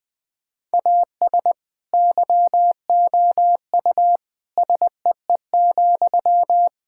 Morse code, heard as a keyed tone, about 20 wpm, 700 Hz. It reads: AS YOU SEE,
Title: Sentences from Top 1000 Words - Review Part 6 - 20wpm